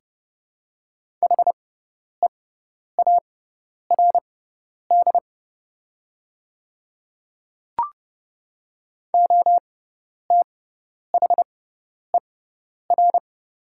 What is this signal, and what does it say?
Morse code, heard as a keyed tone, about 30 wpm, spaced as for 10 wpm, 700 Hz. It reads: HEARD OTHER
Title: 6x Character Spacing - Top 500 Words - 30wpm